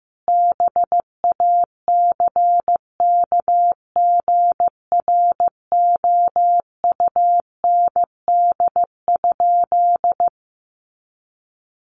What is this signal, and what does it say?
Morse code, heard as a keyed tone, about 15 wpm, 700 Hz. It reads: BACKGROUND?
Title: Sentences from Top 1300 Words 15wpm